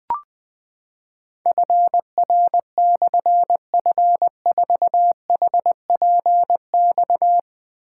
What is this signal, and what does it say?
Morse code, heard as a keyed tone, about 21 wpm, 700 Hz. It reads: FR/F4HPX
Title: Difficult Call Signs 20wpm